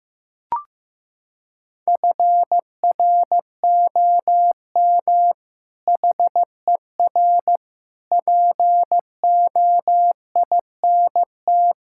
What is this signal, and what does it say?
Morse code, heard as a keyed tone, about 15 wpm, 700 Hz. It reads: FROM HER POINT
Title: Sentences from Top 1000 Words - Review Part 6 - 15wpm